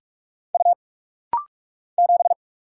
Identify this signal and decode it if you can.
Morse code, heard as a keyed tone, about 44 wpm, 700 Hz. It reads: U 7
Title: Single Letter-Number - Rapid-Fire 45wpm